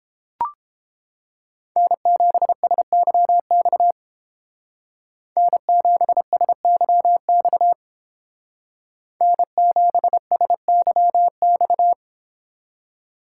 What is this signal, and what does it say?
Morse code, heard as a keyed tone, about 28 wpm, 700 Hz. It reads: N7SYX N7SYX N7SYX